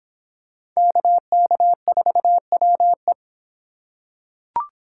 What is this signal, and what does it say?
Morse code, heard as a keyed tone, about 26 wpm, 700 Hz. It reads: KK4WE